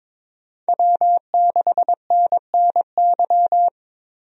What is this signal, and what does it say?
Morse code, heard as a keyed tone, about 21 wpm, 700 Hz. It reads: W6NNY
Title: USA Call Signs - 5 Characters - Repeated 3x - 22wpm